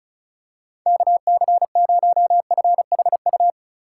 Morse code, heard as a keyed tone, about 34 wpm, 700 Hz. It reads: KC0FHU
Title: USA Call Signs - 6 Characters - 2x Character Spacing - 35wpm